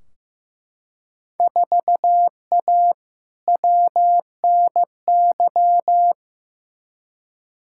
Morse code, heard as a keyed tone, about 15 wpm, 700 Hz. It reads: 4A WNY
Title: Field Day Exchanges Encore - 15wpm